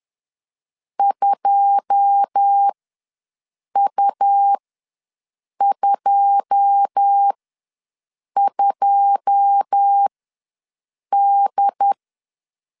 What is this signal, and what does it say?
Morse code, heard as a keyed tone, about 10 wpm, spaced as for 6 wpm, 800 Hz. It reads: 2U22D
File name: LEZIONI / 9 2 ws 10wpm 4130